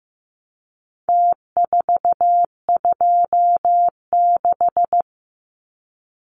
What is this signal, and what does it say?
Morse code, heard as a keyed tone, about 15 wpm, 700 Hz. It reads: T426
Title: Numbers - Contest Style - 4-Digit 15wpm